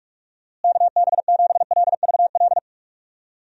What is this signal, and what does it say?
Morse code, heard as a keyed tone, about 46 wpm, 700 Hz. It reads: KB7LVL
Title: USA Call Signs - 6 Characters - Repeated 3x - 45wpm